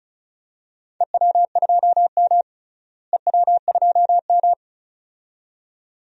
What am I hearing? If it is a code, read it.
Morse code, heard as a keyed tone, about 34 wpm, 700 Hz. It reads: EW2M EW2M
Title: DXCC-100 Call Signs - 4 Characters - Repeated 2x - 35wpm